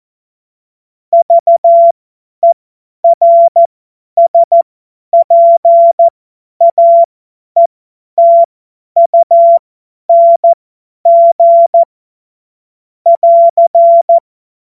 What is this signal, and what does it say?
Morse code, heard as a keyed tone, about 14 wpm, spaced as for 10 wpm, 680 Hz. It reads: VERSPAETUNG <AR>
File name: Morsekurs / Witze / Witz 20 Februar 2024 50 F70